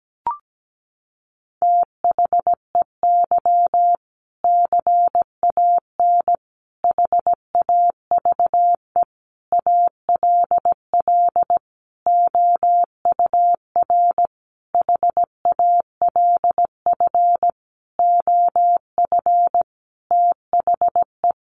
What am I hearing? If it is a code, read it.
Morse code, heard as a keyed tone, about 17 wpm, 700 Hz. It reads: THEY CAN HAVE ALL OUR HALF OF THE
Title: Sentences from Top 400 Words 17wpm